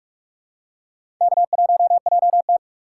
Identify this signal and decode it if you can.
Morse code, heard as a keyed tone, about 46 wpm, 700 Hz. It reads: K1JT
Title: USA Call Signs - 4 Characters - 3x Character Spacing - 45wpm